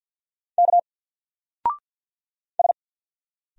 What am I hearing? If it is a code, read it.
Morse code, heard as a keyed tone, about 50 wpm, 700 Hz. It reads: K S
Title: Single Character (Letter:Number:Punctuation:Prosign) 50wpm